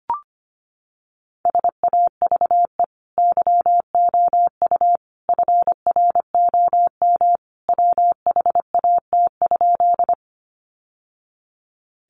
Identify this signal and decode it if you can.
Morse code, heard as a keyed tone, about 25 wpm, 700 Hz. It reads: SAVE YOU FROM WHAT?
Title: Sentences from Top 1100 Words 25wpm